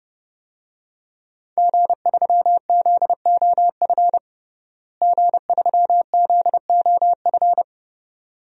Morse code, heard as a keyed tone, about 29 wpm, 700 Hz. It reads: G3ZOF G3ZOF
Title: DXCC-100 Call Signs - 5 Characters - Repeated 2x - 30wpm